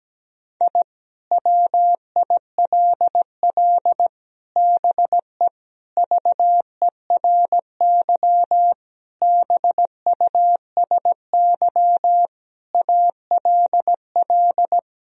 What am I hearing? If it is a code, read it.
Morse code, heard as a keyed tone, about 17 wpm, 700 Hz. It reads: I WILL BE VERY BUSY ALL
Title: Sentences from Top 1000 Words - Review Part 3 - 17wpm